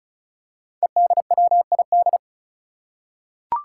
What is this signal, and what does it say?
Morse code, heard as a keyed tone, about 36 wpm, 700 Hz. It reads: EDWID
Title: Five Challenging Letter-Number Combinations 35wpm